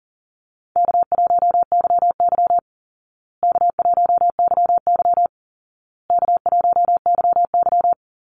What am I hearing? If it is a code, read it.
Morse code, heard as a keyed tone, about 40 wpm, 700 Hz. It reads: K1YY K1YY K1YY
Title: Easy Call Signs Repeated 3x - 40wpm